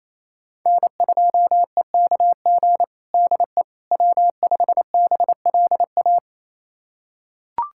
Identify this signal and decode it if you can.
Morse code, heard as a keyed tone, about 28 wpm, 700 Hz. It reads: N2EKG DE W5BLA
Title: USA Callsign DE Callsign - 5 Characters - 28wpm